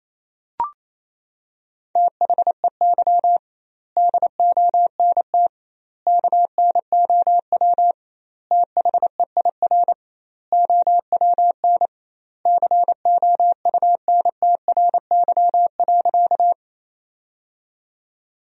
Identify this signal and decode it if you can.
Morse code, heard as a keyed tone, about 28 wpm, 700 Hz. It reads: THEY DONT KNOW THEIR OWN COUNTRY.
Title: 2x Word-Spacing - Sentences from Top 300 Words - 28wpm